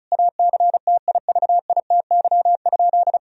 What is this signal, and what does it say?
Morse code, heard as a keyed tone, about 34 wpm, 700 Hz. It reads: ACTIVITY?
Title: Sentences from Top 1000 Words - Review Part 4 - 35wpm